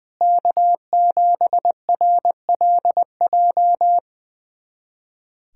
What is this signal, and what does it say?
Morse code, heard as a keyed tone, about 20 wpm, 700 Hz. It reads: K7RLJ
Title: USA Callsign DE Callsign - 5 Characters - 20wpm